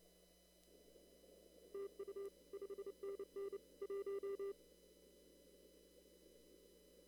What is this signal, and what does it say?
Morse code, heard as a keyed tone, about 30 wpm, 400 Hz. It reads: TU 5NN 1
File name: CW / kt5j wpx cw-op-k5pi